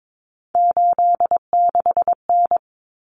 Morse code, heard as a keyed tone, about 22 wpm, 700 Hz. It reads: 86N
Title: Common Cut Numbers - 3-Digit 22wpm